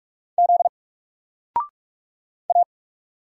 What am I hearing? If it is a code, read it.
Morse code, heard as a keyed tone, about 46 wpm, 700 Hz. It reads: Z A
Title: Single Character (Letter:Number:Punctuation:Prosign) Alt 45wpm